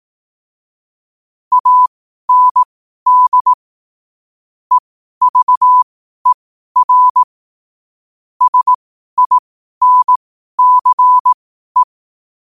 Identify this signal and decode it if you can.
Morse code, heard as a keyed tone, about 18 wpm, spaced as for 12 wpm, 1000 Hz. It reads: AND EVER SINCE
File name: qotdfiles / 12wpm